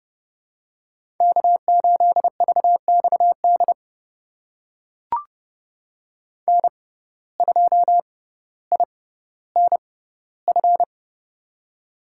Morse code, heard as a keyed tone, about 30 wpm, spaced as for 15 wpm, 700 Hz. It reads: K8VXD N2INF
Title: USA Call Signs - 5 Characters - 6x Character Spacing - 30wpm